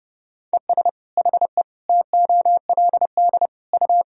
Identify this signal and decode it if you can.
Morse code, heard as a keyed tone, about 30 wpm, 700 Hz. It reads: ES HE TOLD U
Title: Sentences from Top 500 Words and Abbreviations 30wpm